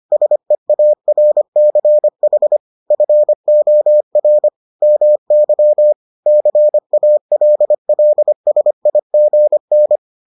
Morse code, heard as a keyed tone, about 25 wpm, 600 Hz. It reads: SEARCH FOR MY CALLSIGN